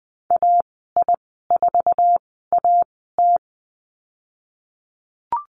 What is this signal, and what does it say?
Morse code, heard as a keyed tone, about 20 wpm, spaced as for 15 wpm, 700 Hz. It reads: AI4AT
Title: 055 Lesson 10g - Letter r - Call Signs 20wpm